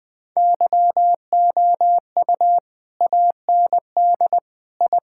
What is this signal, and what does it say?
Morse code, heard as a keyed tone, about 20 wpm, 700 Hz. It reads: YOU AND I
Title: Sentences with 2-Word Phrases - 20wpm